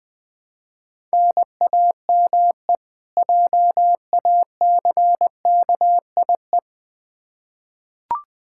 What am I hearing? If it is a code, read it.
Morse code, heard as a keyed tone, about 20 wpm, 700 Hz. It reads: NAME JACKIE
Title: Practice QSO - Exchange 1 - Beginning - 20wpm